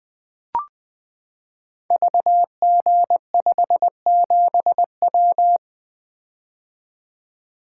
Morse code, heard as a keyed tone, about 20 wpm, 700 Hz. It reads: VG57W
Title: Code Groups - 5 Letter:Numbers - 20wpm